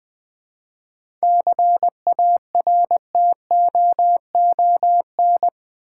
Morse code, heard as a keyed tone, about 20 wpm, 690 Hz. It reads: CARTOON
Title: 7 Letter Words - First 3 Letters Given - 20wpm